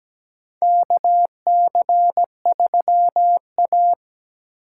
Morse code, heard as a keyed tone, about 17 wpm, 700 Hz. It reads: KC3A
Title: USA Call Signs - 4x Character Spacing - 17wpm